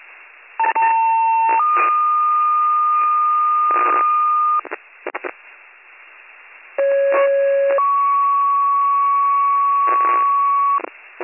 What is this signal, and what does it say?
Two-tone page, heard as a one-second tone, then a three-second tone, about 900 then 1220 Hz, about 550 then 1120 Hz. It is heard from 0.6 to 10.8 s.